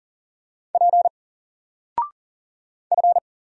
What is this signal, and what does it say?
Morse code, heard as a keyed tone, about 40 wpm, 700 Hz. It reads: P F
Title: Single Character (Letter:Number:Punctuation:Prosign) Alt 40wpm